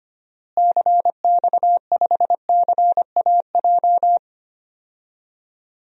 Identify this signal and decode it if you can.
Morse code, heard as a keyed tone, about 25 wpm, 700 Hz. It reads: CX5CAJ
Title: DXCC-100 Call Signs - 6 Characters - Repeated 2x - 25wpm